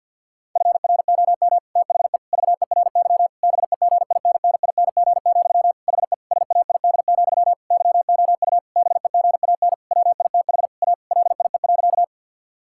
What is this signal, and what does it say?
Morse code, heard as a keyed tone, about 50 wpm, 700 Hz. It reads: FROM THE VERY BEGINNING, HE SAID, YOU BEGAN WITH A LIE.
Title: Sentences from Top 1600 Words 50wpm